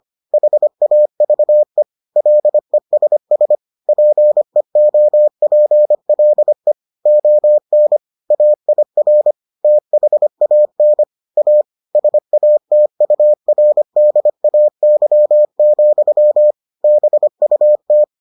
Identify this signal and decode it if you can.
Morse code, heard as a keyed tone, about 25 wpm, 600 Hz. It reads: HAVE LESS PEOPLE ON AIR THAN A SATURDAY, BUT